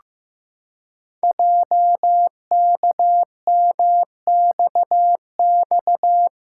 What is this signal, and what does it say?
Morse code, heard as a keyed tone, about 15 wpm, 700 Hz. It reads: JKMXX
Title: Code Groups - 5 Letters - 15wpm